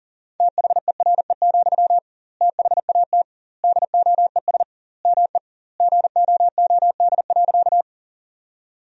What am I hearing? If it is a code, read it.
Morse code, heard as a keyed tone, about 40 wpm, 700 Hz. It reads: THERE, THAT DOES ME GOOD.
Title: 2x Word-Spacing - Sentences from Top 200 Words - 40wpm